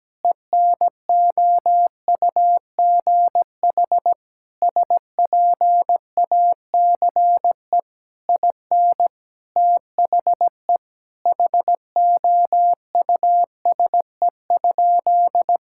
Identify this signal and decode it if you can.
Morse code, heard as a keyed tone, about 17 wpm, 700 Hz. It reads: ENOUGH SPACE IN THE HOUSE?
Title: Sentences from Top 1000 Words - Review Part 5 - 17wpm